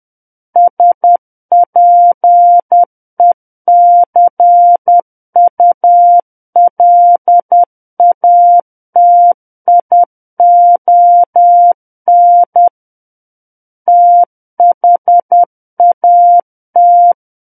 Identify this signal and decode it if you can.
Morse code, heard as a keyed tone, about 10 wpm, 700 Hz. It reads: SPECULATION THAT
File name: Book One / 01-Chapter 1 - 10 WPM